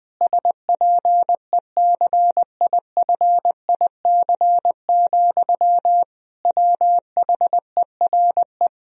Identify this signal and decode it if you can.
Morse code, heard as a keyed tone, about 21 wpm, 700 Hz. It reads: SPECIFIC, WHERE
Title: Sentences from Top 1000 Words - Review Part 3 - 20wpm